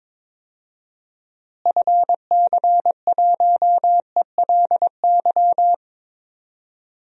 Morse code, heard as a keyed tone, about 22 wpm, 700 Hz. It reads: FC1ELY
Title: DXCC-100 Call Signs - 6 Characters - Repeated 2x - 22wpm